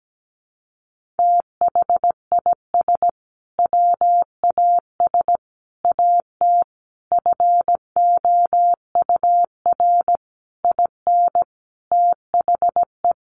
Morse code, heard as a keyed tone, about 17 wpm, 700 Hz. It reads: THIS WAS AT FOUR IN THE
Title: Sentences from Top 1500 Words 17wpm